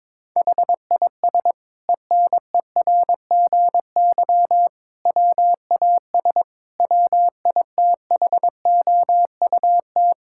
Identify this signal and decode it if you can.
Morse code, heard as a keyed tone, about 21 wpm, 700 Hz. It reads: HIS ENERGY WAS WITHOUT
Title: Sentences from Top 1000 Words - Review Part 6 - 22wpm